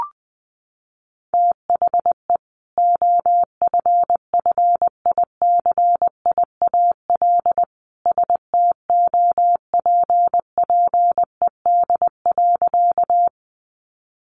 Morse code, heard as a keyed tone, about 20 wpm, 700 Hz. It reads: THE OFFICIAL STOPPED.
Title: A Sound Of Thunder 20wpm